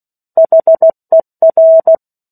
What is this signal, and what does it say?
Morse code, heard as a keyed tone, about 16 wpm, 650 Hz. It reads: HER
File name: 03-Chapter-03 16 WPM